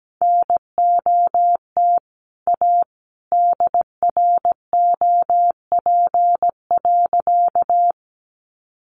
Morse code, heard as a keyed tone, about 17 wpm, 700 Hz. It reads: NOT A DROP.
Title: Sentences from Top 1400 Words 17wpm